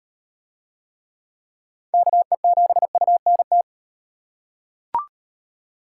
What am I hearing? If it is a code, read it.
Morse code, heard as a keyed tone, about 37 wpm, 700 Hz. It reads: KE7UNT